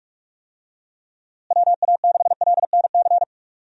Morse code, heard as a keyed tone, about 44 wpm, 700 Hz. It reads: WA6LNC
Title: USA Call Signs - 6 Characters - Repeated 3x - 45wpm